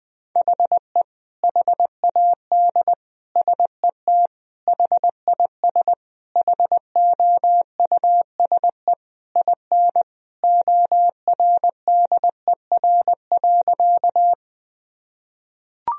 Morse code, heard as a keyed tone, about 21 wpm, 700 Hz. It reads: HE HAD SET HIS HOUSE IN ORDER.
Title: Sentences from Top 1000 Words - Review Part 2 - 20wpm